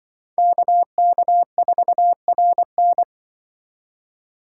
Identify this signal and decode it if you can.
Morse code, heard as a keyed tone, about 23 wpm, 700 Hz. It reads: KK4RN
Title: Speed Racing - USA 2x2 Call Signs 20wpm